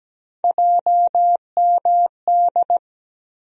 Morse code, heard as a keyed tone, about 17 wpm, 700 Hz. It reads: JMD
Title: USA Call Sign Postfixes - 3 Letters 17wpm